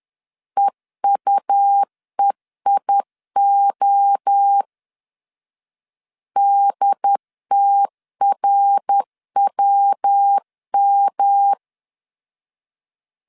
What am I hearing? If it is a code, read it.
Morse code, heard as a keyed tone, about 10 wpm, 800 Hz. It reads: EUEIO DTRWM
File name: CW / DET / 1 3 1500C S C 10wpm 3335